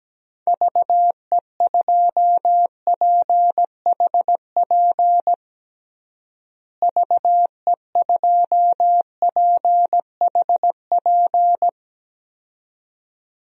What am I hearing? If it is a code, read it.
Morse code, heard as a keyed tone, about 17 wpm, 700 Hz. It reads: VE2PHP VE2PHP
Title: DXCC-100 Call Signs - 6 Characters - Repeated 2x - 17wpm